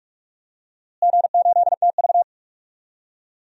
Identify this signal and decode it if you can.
Morse code, heard as a keyed tone, about 46 wpm, 700 Hz. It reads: G8TV
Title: DXCC-100 Call Signs - 4 Characters - Repeated 3x - 45wpm